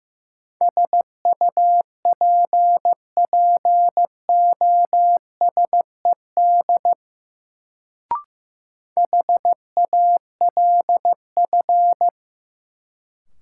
Morse code, heard as a keyed tone, about 16 wpm, 700 Hz. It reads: SUPPOSED HALF
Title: Sets of 3 Words - New Word Drill - 15wpm